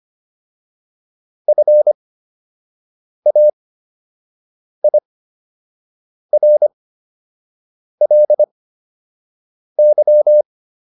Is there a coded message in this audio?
Yes: Morse code from 1.5 to 10.4 s, heard as a keyed tone, about 25 wpm, spaced as for 6 wpm, 600 Hz. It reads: FAIRLY